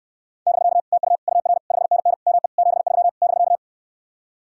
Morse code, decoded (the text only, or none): <BK> TU UR 5NN ME 73 <BK>